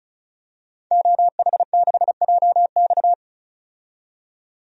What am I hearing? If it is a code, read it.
Morse code, heard as a keyed tone, about 36 wpm, 700 Hz. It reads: OH6JX
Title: DXCC-100 Call Signs - 5 Characters - 35wpm